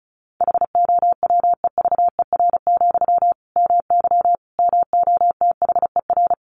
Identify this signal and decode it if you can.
Morse code, heard as a keyed tone, about 34 wpm, 700 Hz. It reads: HOWEVER, MY MOTHER
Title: Sentences from Top 1000 Words 35wpm